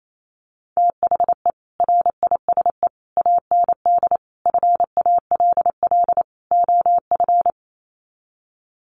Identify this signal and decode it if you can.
Morse code, heard as a keyed tone, about 28 wpm, 700 Hz. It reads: THE RISE AND FALL OF
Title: Sets of 5 Words Encore2 28wpm